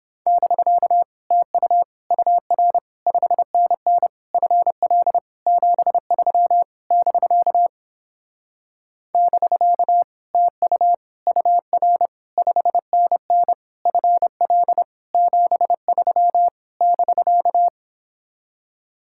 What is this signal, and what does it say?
Morse code, heard as a keyed tone, about 29 wpm, 700 Hz. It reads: <BK> TU UR 5NN FL 73 <BK> <BK> TU UR 5NN FL 73 <BK>